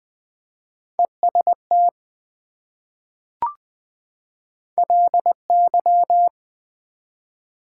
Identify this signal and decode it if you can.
Morse code, heard as a keyed tone, about 20 wpm, 700 Hz. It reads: EST LY
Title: Prefixes and Suffixes 20wpm